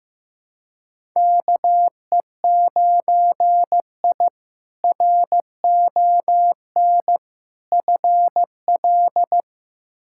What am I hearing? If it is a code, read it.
Morse code, heard as a keyed tone, about 15 wpm, 700 Hz. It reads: KE9I RON FL
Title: K1USN SST QSO Practice - 15wpm